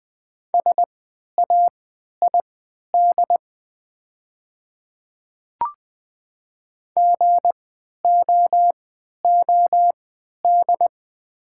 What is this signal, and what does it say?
Morse code, heard as a keyed tone, about 20 wpm, spaced as for 11 wpm, 700 Hz. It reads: SAID GOOD
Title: 3x Character Spacing - Top 200 Words - 20wpm